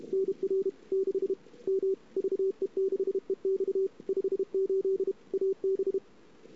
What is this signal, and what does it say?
Morse code, heard as a keyed tone, about 34 wpm, 410 Hz. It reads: NR 6 M VE6EX 58 AB
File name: CW / 2008-ss-cw-k5tr-op-n5rz -21:00